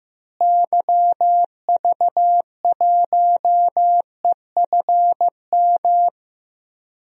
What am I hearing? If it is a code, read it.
Morse code, heard as a keyed tone, about 15 wpm, 700 Hz. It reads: YV1EFM
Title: DXCC-100 Call Signs - 6 Characters - Repeated 3x - 15wpm